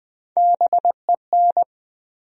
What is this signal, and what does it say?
Morse code, heard as a keyed tone, about 20 wpm, 700 Hz. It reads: BEN
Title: USA Common Names - Max 3 Letters - 20wpm